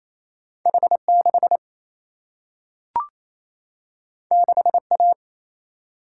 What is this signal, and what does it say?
Morse code, heard as a keyed tone, about 28 wpm, 700 Hz. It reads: H 6 6 A
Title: Single-Letter Single-Number 28wpm